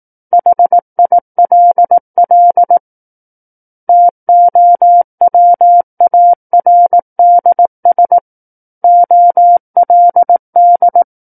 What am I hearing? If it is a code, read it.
Morse code, heard as a keyed tone, about 18 wpm, 700 Hz. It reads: HILL TOWARDS OLD
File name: 18 WPM